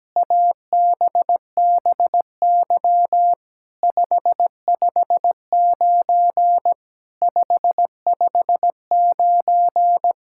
Morse code, heard as a keyed tone, about 17 wpm, 700 Hz. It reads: ABBY 559 559
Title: SOTA QSO Practice 17wpm